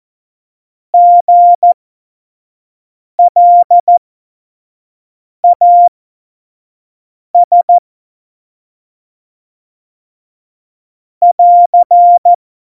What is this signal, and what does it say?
Morse code, heard as a keyed tone, about 14 wpm, spaced as for 5 wpm, 700 Hz. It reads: GLAS <AR>
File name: Morsekurs / Witze / Witz 24.Dezember 2019, 25 F70